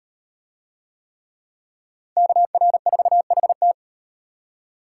Morse code, heard as a keyed tone, about 38 wpm, 700 Hz. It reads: KR4HT